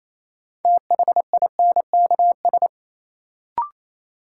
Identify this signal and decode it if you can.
Morse code, heard as a keyed tone, about 28 wpm, 700 Hz. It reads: THINKS
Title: Sentences from Top 1000 Words - Review Part 6 - New Word Drill - 28wpm